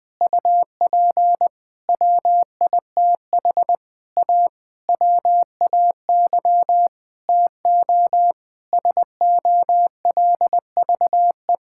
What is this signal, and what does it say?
Morse code, heard as a keyed tone, about 20 wpm, 700 Hz. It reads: UP WITH A WAY TO SOLVE
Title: Sentences with 7-Word Phrases - 20wpm